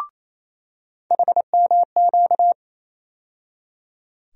Morse code, heard as a keyed tone, about 27 wpm, 700 Hz. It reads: HMQ
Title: USA Call Sign Postfixes - 3 Letters 28wpm